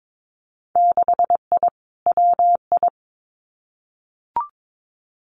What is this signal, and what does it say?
Morse code, heard as a keyed tone, about 22 wpm, 700 Hz. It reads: 6I WI
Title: Winter Field Day Exchanges Encore 22wpm